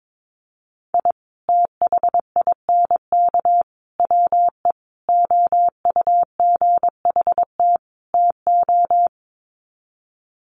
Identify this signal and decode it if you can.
Morse code, heard as a keyed tone, about 22 wpm, 700 Hz. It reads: I THINK WE OUGHT TO
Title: Sets of 5 Words Encore 22wpm